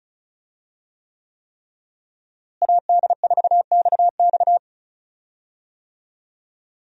Morse code, heard as a keyed tone, about 35 wpm, 700 Hz. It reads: AD4XX
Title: USA Call Signs - 5 Characters - Repeated 3x - 35wpm